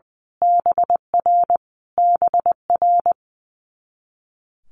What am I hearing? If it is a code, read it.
Morse code, heard as a keyed tone, about 20 wpm, 700 Hz. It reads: BR BR
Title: Multi-letter Phonemes Repeated 3x 20wpm